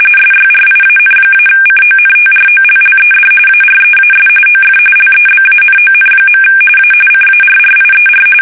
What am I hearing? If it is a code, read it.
RTTY, about 45.45 baud. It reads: SYRIAN-CONTROLLED AREAS OF LEBANON. A CARRIER
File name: fsk850 / battleshipnj